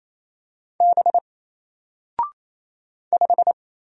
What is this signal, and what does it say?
Morse code, heard as a keyed tone, about 28 wpm, 700 Hz. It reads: B 5
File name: Single-Letter-Number-28wpm